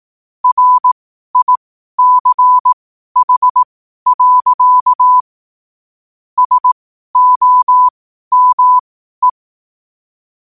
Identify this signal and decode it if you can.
Morse code, heard as a keyed tone, about 18 wpm, spaced as for 12 wpm, 1000 Hz. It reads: RICH. SOME
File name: qotdfiles / 12wpm